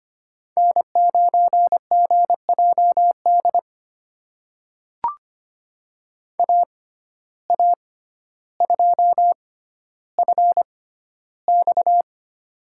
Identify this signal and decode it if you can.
Morse code, heard as a keyed tone, about 25 wpm, spaced as for 13 wpm, 700 Hz. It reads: N9GJD AA2FX
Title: USA Call Signs - 5 Characters - 6x Character Spacing - 25wpm